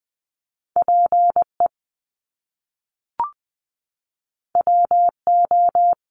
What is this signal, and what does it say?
Morse code, heard as a keyed tone, about 20 wpm, 700 Hz. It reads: P E W O
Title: Top Two Letter Combinations in Words 20wpm